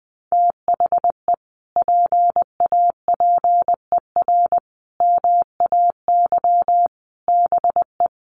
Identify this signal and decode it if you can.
Morse code, heard as a keyed tone, about 20 wpm, 700 Hz. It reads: THE PAPER MAY BE
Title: Sentences from Top 1400 Words 20wpm